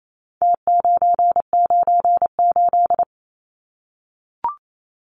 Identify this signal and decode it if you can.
Morse code, heard as a keyed tone, about 28 wpm, 700 Hz. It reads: T998
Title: Numbers - Contest Style - 4-Digit 28wpm